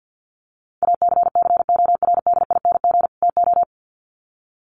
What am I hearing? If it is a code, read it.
Morse code, heard as a keyed tone, about 50 wpm, 700 Hz. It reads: ACCORDING TO
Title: Sets of 2 Words 50wpm